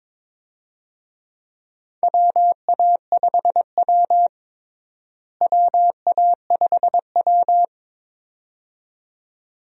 Morse code, heard as a keyed tone, about 22 wpm, 700 Hz. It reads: WA5W WA5W
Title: USA Call Signs - 4 Characters - Repeated 2x - 22wpm